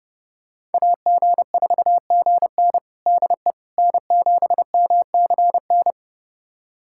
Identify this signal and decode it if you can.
Morse code, heard as a keyed tone, about 32 wpm, 700 Hz. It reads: AG4GN DE N7MCN
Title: USA Callsign DE Callsign - 5 Characters - 30wpm